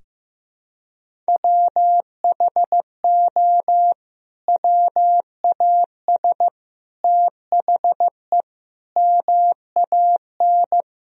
Morse code, heard as a keyed tone, about 16 wpm, 700 Hz. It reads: WHO WAS THE MAN
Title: Sentences from Top 1000 Words - Review Part 1 - 15wpm